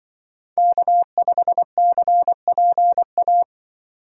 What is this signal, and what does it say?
Morse code, heard as a keyed tone, about 26 wpm, 700 Hz. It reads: K5CPA